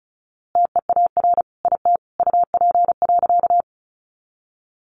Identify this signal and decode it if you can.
Morse code, heard as a keyed tone, about 34 wpm, 700 Hz. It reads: TEAR IT UP.
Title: A Sound Of Thunder 35wpm